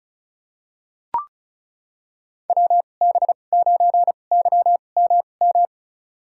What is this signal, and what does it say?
Morse code, heard as a keyed tone, about 34 wpm, spaced as for 25 wpm, 700 Hz. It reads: WB9YMM
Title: USA Call Signs - 6 Characters - 2x Character Spacing - 35wpm